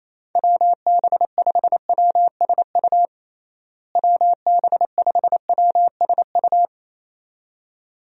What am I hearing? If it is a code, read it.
Morse code, heard as a keyed tone, about 27 wpm, 700 Hz. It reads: WB5WSU WB5WSU
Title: USA Call Signs - 6 Characters - Repeated 2x - 28wpm